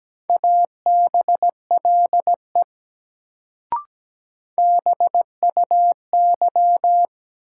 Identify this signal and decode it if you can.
Morse code, heard as a keyed tone, about 17 wpm, 700 Hz. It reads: ABLE BUY
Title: Sentences from Top 700 Words - New Word Drill - 17wpm